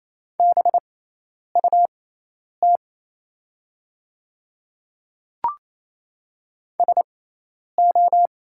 Morse code, heard as a keyed tone, about 27 wpm, spaced as for 10 wpm, 700 Hz. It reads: BUT SO
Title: 6x Character Spacing - Top 100 Words - 28wpm